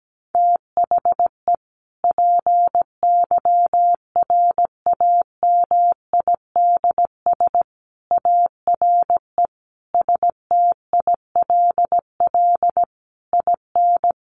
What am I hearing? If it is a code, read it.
Morse code, heard as a keyed tone, about 17 wpm, 700 Hz. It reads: THE PYRAMIDS ARE STILL IN
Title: A Sound Of Thunder 17wpm